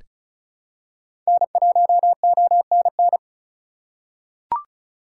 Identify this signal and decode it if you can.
Morse code, heard as a keyed tone, about 34 wpm, 700 Hz. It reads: N1ONN
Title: USA Call Signs - 5 Characters - 4x Character Spacing - 35wpm